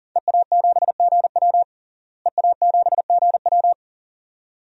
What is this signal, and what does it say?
Morse code, heard as a keyed tone, about 40 wpm, 700 Hz. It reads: EA7GW EA7GW
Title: DXCC-100 Call Signs - 5 Characters - Repeated 2x - 40wpm